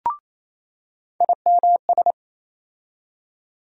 Morse code, heard as a keyed tone, about 28 wpm, 700 Hz. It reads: IMS
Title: USA Call Sign Postfixes - 3 Letters 28wpm